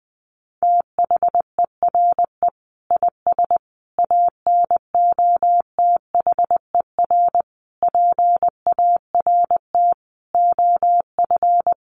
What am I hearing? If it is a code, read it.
Morse code, heard as a keyed tone, about 20 wpm, 700 Hz. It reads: THERE IS ANOTHER PART OF
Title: Sentences from Top 1300 Words 20wpm